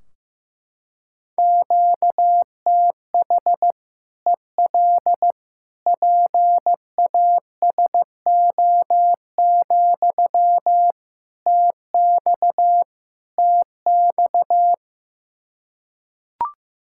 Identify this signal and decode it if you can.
Morse code, heard as a keyed tone, about 15 wpm, 700 Hz. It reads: QTH EL PASO, TX TX
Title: Practice QSO - Exchange 1 - Beginning - 15wpm